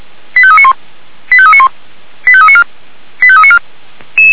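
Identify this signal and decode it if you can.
Five-tone selective call (ZVEI), heard as quick runs of five tones, tones of about 70 ms. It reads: 85391 85391 85394 85394
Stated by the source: ZVEI five-tone call 89135 received by SDR.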